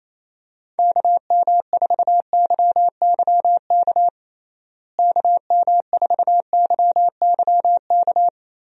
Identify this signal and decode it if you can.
Morse code, heard as a keyed tone, about 28 wpm, 700 Hz. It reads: KM4YYK KM4YYK
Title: USA Call Signs - 6 Characters - Repeated 2x - 28wpm